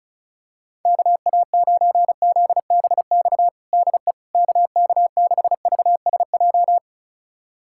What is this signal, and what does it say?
Morse code, heard as a keyed tone, about 34 wpm, 700 Hz. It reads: KA9ZBX DE KK6VSJ
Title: USA Callsign DE Callsign - 6 Characters - 35wpm